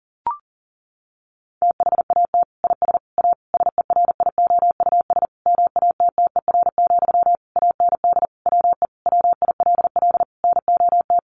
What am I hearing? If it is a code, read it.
Morse code, heard as a keyed tone, about 40 wpm, 700 Hz. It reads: THAT IS A SERIOUS MATTER, AND WE WILL NOT
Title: Sentences from Top 1100 Words 40wpm